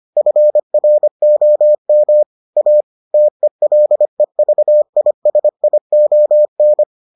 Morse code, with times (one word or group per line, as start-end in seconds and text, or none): 0.2-2.2 FROM
2.6-2.8 A
3.1-6.8 TELEVISION